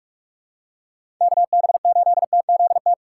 Morse code, heard as a keyed tone, about 44 wpm, 700 Hz. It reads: KB8TZT
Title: USA Call Signs - 6 Characters - 6x Character Spacing - 45wpm